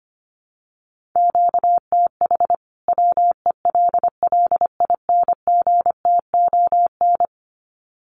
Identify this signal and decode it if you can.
Morse code, heard as a keyed tone, about 25 wpm, 700 Hz. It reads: QTH WELLINGTON
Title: Common QSO Phrases 25wpm